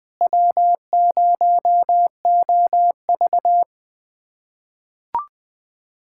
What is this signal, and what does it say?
Morse code, heard as a keyed tone, about 20 wpm, 700 Hz. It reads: W0OV
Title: USA Callsign DE Callsign - 4 Characters - 20wpm